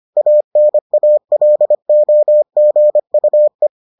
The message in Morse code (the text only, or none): ANALOGUE